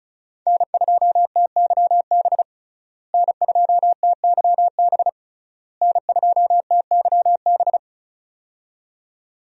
Morse code, heard as a keyed tone, about 36 wpm, 700 Hz. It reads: N2TYB N2TYB N2TYB
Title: USA Call Signs - 5 Characters - Repeated 3x - 35wpm